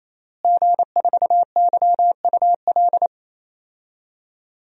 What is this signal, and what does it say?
Morse code, heard as a keyed tone, about 28 wpm, 700 Hz. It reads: G4YUL
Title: DXCC-100 Call Signs - 5 Characters - Repeated 3x - 28wpm